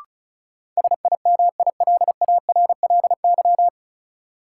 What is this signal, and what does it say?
Morse code, heard as a keyed tone, about 34 wpm, 700 Hz. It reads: SIMILARLY
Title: Sentences from Top 1400 Words - New Word Drill - 35wpm